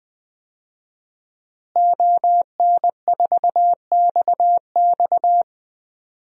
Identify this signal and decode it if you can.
Morse code, heard as a keyed tone, about 20 wpm, 700 Hz. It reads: ON4XX
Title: DXCC-100 Call Signs - 5 Characters - Repeated 2x - 20wpm